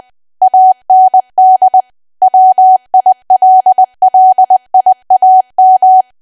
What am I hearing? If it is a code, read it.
Morse code, heard as a keyed tone, about 20 wpm, 750 Hz. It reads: AND WILLIAM
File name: cwa / W1AW-Level3 / 20WPM